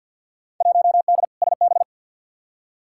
Morse code, heard as a keyed tone, about 52 wpm, 700 Hz. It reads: 1D SB
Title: Field Day Exchanges - 50wpm